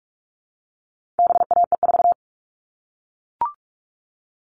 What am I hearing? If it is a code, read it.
Morse code, heard as a keyed tone, about 44 wpm, 700 Hz. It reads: BAE4
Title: All Cut Numbers - 4-Digit 45wpm